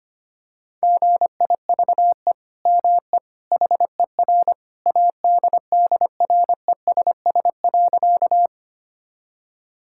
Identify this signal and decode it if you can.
Morse code, heard as a keyed tone, about 25 wpm, 700 Hz. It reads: GIVE ME HER ADDRESS.
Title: Sentences from Top 1000 Words - Review Part 6 - 25wpm